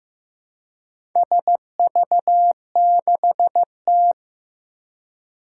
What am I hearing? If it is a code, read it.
Morse code, heard as a keyed tone, about 15 wpm, 700 Hz. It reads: SV6T
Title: Four Challenging Letter-Number Combinations 15wpm